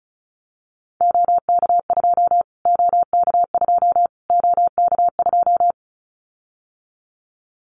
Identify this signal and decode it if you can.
Morse code, heard as a keyed tone, about 34 wpm, 700 Hz. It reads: OK2 OK2 OK2
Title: Callsign Prefixes for Top 25 DXCC - Repeated 3x - 35wpm